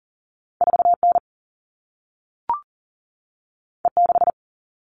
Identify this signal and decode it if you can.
Morse code, heard as a keyed tone, about 42 wpm, 700 Hz. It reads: 4N E6
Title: All Cut Numbers - 2-Digit 40wpm